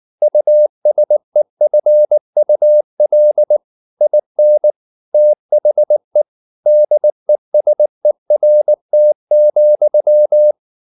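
Morse code, heard as a keyed tone, about 19 wpm, 600 Hz. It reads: USEFUL IN THE DESERT,
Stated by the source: Morse code at 19 wpm from CW med Gnister, reading Ken Follett's novel Never in English.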